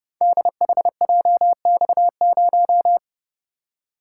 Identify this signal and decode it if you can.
Morse code, heard as a keyed tone, about 30 wpm, 700 Hz. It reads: DHJX0
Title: Code Groups - 5 Letter:Numbers - 30wpm